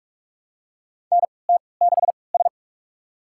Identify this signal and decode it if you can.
Morse code, heard as a keyed tone, about 47 wpm, spaced as for 26 wpm, 700 Hz. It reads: NT6S
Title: USA Call Signs - 4 Characters - 3x Character Spacing - 45wpm